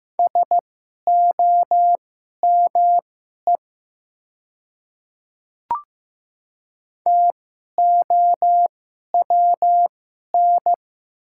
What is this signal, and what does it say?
Morse code, heard as a keyed tone, about 15 wpm, spaced as for 11 wpm, 700 Hz. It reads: SOME TOWN
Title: 2x Character Spacing - Top 500 Words - 15wpm